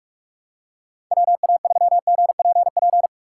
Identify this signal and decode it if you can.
Morse code, heard as a keyed tone, about 46 wpm, 700 Hz. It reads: WA3GPP
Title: K1USN SST QSO Practice - 45wpm